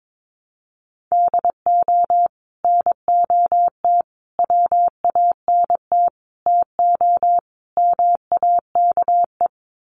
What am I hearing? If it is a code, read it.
Morse code, heard as a keyed tone, about 22 wpm, 700 Hz. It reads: DO NOT WANT TO MAKE
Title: Sets of 5 Words Encore 22wpm